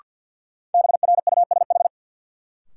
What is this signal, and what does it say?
Morse code, heard as a keyed tone, about 52 wpm, 700 Hz. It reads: BRUSH
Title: Sentences from Top 1100 Words - New Word Drill Alt - 50wpm